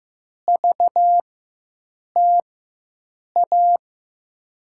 Morse code, heard as a keyed tone, about 15 wpm, spaced as for 7 wpm, 700 Hz. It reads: VTA
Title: USA Call Signs - 4x Character Spacing - 15wpm